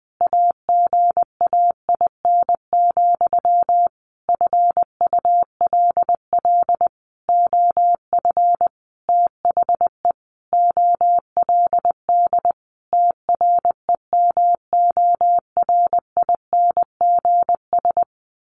Morse code, heard as a keyed tone, about 20 wpm, 700 Hz. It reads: AGAIN, FULL OF THE OLD TREMORINGS
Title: A Sound Of Thunder 20wpm